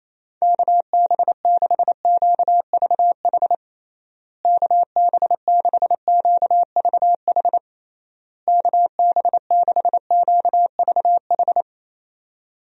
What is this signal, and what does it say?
Morse code, heard as a keyed tone, about 28 wpm, 700 Hz. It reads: KB6QVH KB6QVH KB6QVH
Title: USA Call Signs - 6 Characters - Repeated 3x - 28wpm